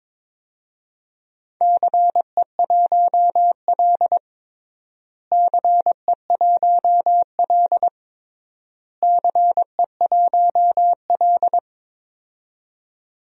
Morse code, heard as a keyed tone, about 22 wpm, 700 Hz. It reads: CE1L CE1L CE1L
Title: DXCC-100 Call Signs - 4 Characters - Repeated 3x - 22wpm